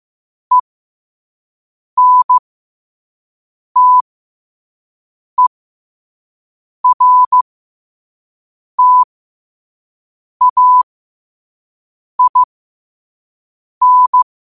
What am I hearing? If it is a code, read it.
Morse code, heard as a keyed tone, about 15 wpm, spaced as for 5 wpm, 1000 Hz. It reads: ENTERTAIN
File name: qotdfiles / 5wpm